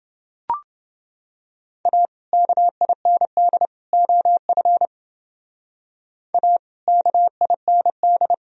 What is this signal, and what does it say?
Morse code, heard as a keyed tone, about 28 wpm, 700 Hz. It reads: A KIND OF A KIND